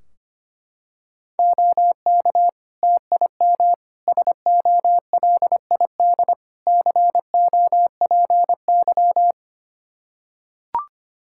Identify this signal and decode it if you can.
Morse code, heard as a keyed tone, about 25 wpm, 700 Hz. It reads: OK TIM SOLID COPY
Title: Practice QSO - Exchange 3 - More Info - 25wpm